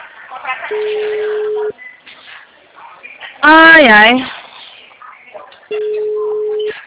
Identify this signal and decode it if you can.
Ringback tone (Europe, 425 Hz), with speech.
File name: out- -2015- - - .20649